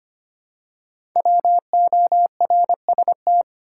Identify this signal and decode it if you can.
Morse code, heard as a keyed tone, about 26 wpm, 700 Hz. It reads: WORST
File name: 5 Letter Words - First 3 Letters Given - 25wpm